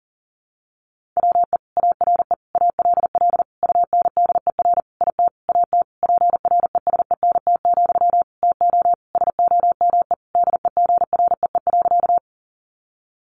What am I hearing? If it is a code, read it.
Morse code, heard as a keyed tone, about 40 wpm, 700 Hz. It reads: WE ARE ALL UNDER IT AT PRESENT, TO SOME DEGREE.
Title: Sentences from Top 1500 Words 40wpm